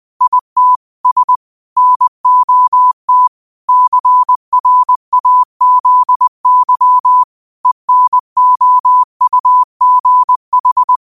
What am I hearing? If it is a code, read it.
Morse code, heard as a keyed tone, about 20 wpm, 1000 Hz. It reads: IT S NOT CRAZY ENOUGH
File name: qotdfiles / 20wpm